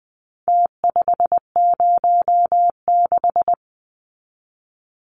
Morse code, heard as a keyed tone, about 20 wpm, 700 Hz. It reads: T506
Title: Numbers - Contest Style - 4-Digit 20wpm